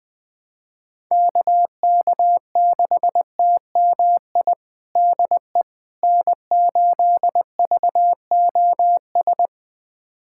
Morse code, heard as a keyed tone, about 21 wpm, 700 Hz. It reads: KK6TMI DE N8VOS
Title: USA Callsign DE Callsign 20wpm